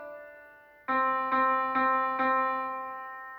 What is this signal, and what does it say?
There is no code.